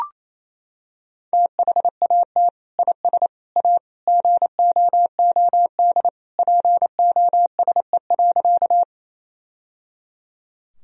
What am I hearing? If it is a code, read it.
Morse code, heard as a keyed tone, about 28 wpm, 700 Hz. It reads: THAT IS A GOOD POSE.
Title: Sentences from Top 1000 Words - Review Part 2 - 28wpm